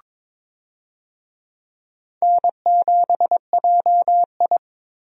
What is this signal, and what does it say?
Morse code, heard as a keyed tone, about 22 wpm, 700 Hz. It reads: N7JI
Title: USA Call Signs - 4 Characters - Repeated 3x - 22wpm